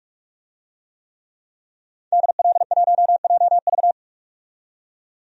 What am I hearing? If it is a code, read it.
Morse code, heard as a keyed tone, about 41 wpm, 700 Hz. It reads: DL1JV
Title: DXCC-100 Call Signs - 5 Characters - 45wpm